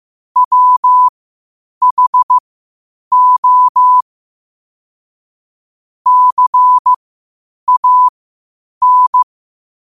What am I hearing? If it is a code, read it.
Morse code, heard as a keyed tone, about 15 wpm, spaced as for 8 wpm, 1000 Hz. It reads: WHO CAN
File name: qotdfiles / 8wpm